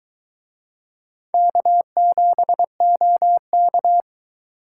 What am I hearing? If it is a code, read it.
Morse code, heard as a keyed tone, about 23 wpm, 700 Hz. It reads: K7OK